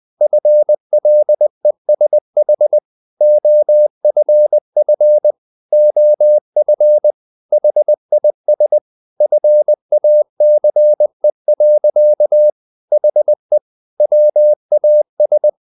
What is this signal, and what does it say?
Morse code, heard as a keyed tone, about 20 wpm, 600 Hz. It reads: FLESH OFF OF HIS FACE. HE WAS
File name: the old man and the sea part 2 20wpm